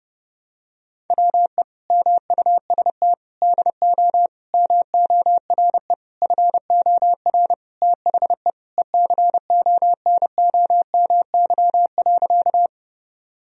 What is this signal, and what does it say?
Morse code, heard as a keyed tone, about 30 wpm, 700 Hz. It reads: WE MUST DO MORE FOR THE ECONOMY.
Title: Sentences from Top 1000 Words - Review Part 3 - 30wpm